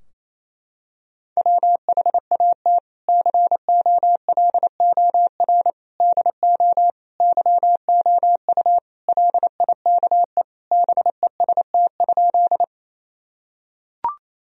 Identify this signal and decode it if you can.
Morse code, heard as a keyed tone, about 28 wpm, 700 Hz. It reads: WHAT COLOR DO YOU LIKE BEST?
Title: Sentences from Top 1000 Words - Review Part 2 - 28wpm